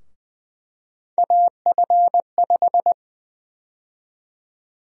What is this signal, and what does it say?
Morse code, heard as a keyed tone, about 20 wpm, 700 Hz. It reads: AF5
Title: USA Call Sign Prefixes - 20wpm